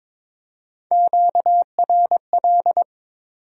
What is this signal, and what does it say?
Morse code, heard as a keyed tone, about 27 wpm, 700 Hz. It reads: QRL
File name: Speed-Racing Top 20 QSO Elements - 22wpm